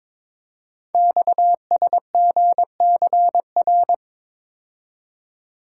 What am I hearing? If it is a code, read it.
Morse code, heard as a keyed tone, about 22 wpm, 700 Hz. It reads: XSGCR
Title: Code Groups - 5 Letters - 22wpm